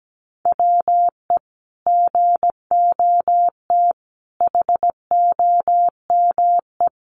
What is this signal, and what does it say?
Morse code, heard as a keyed tone, about 17 wpm, 700 Hz. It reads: WE GOT HOME
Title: Sentences from Top 900 Words 17wpm